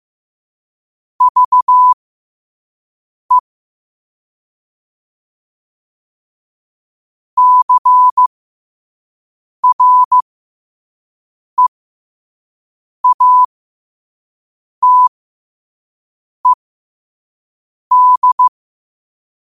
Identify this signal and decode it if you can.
Morse code, heard as a keyed tone, about 15 wpm, spaced as for 5 wpm, 1000 Hz. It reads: VE CREATED
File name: qotdfiles / 5wpm